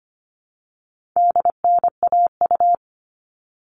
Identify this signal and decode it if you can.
Morse code, heard as a keyed tone, about 25 wpm, 700 Hz. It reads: DNAU
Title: All Cut Numbers - 4-Digit 25wpm